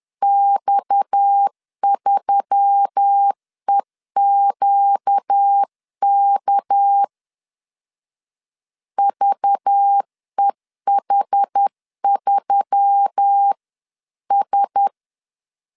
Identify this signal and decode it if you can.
Morse code, heard as a keyed tone, about 10 wpm, 800 Hz. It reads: X3EQK VEH3S
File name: CW / DET / 1 6 1500C S C 10wpm 4252